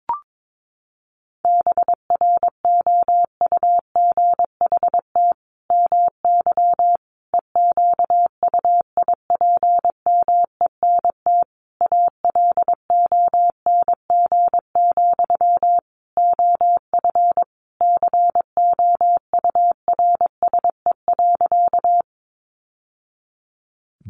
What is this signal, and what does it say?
Morse code, heard as a keyed tone, about 22 wpm, 700 Hz. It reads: BROUGHT MY EQUIPMENT ALONG, OF COURSE.
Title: Sentences from Top 700 Words 22wpm